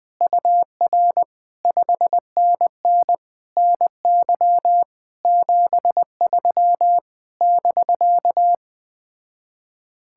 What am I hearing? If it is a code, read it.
Morse code, heard as a keyed tone, about 20 wpm, 700 Hz. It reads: UR 5NN NY 73 <BK>